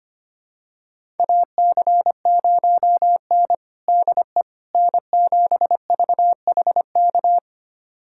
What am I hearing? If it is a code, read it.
Morse code, heard as a keyed tone, about 25 wpm, 700 Hz. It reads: AC0N DE N7VHK
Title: USA Callsign DE Callsign 25wpm